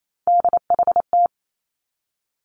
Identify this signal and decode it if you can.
Morse code, heard as a keyed tone, about 28 wpm, 700 Hz. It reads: DHT
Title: Three Challenging Letter-Number Combinations 28wpm